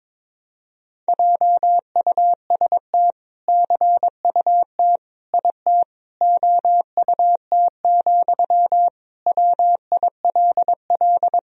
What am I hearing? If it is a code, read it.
Morse code, heard as a keyed tone, about 22 wpm, 700 Hz. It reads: JUST CUT IT OUT, WILL
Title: Sentences from Top 300 Words and Abbreviations 22wpm